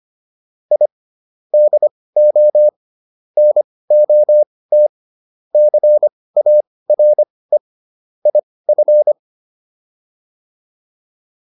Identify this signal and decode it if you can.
Morse code, heard as a keyed tone, about 25 wpm, spaced as for 18 wpm, 600 Hz. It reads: I DO NOT CARE IF